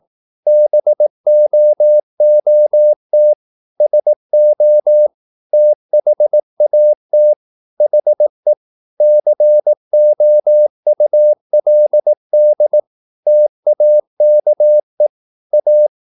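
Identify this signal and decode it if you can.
Morse code, heard as a keyed tone, about 18 wpm, 600 Hz. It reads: BOOT SO THAT HE COULD TAKE A
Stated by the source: Morse practice at 18 wpm, English text, from CW med Gnister.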